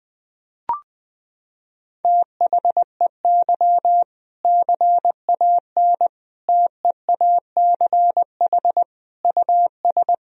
Morse code, heard as a keyed tone, about 20 wpm, 700 Hz. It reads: THEY CAN TEACH US
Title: Sentences from Top 1000 Words - Review Part 2 - 20wpm